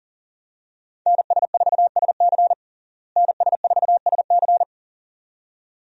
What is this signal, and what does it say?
Morse code, heard as a keyed tone, about 37 wpm, 700 Hz. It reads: NS4SC NS4SC
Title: USA Call Signs - 5 Characters - Repeated 2x - 40wpm